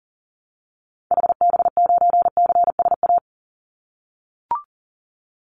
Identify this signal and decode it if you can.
Morse code, heard as a keyed tone, about 41 wpm, 700 Hz. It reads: HB9CSA
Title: Call Signs - 40wpm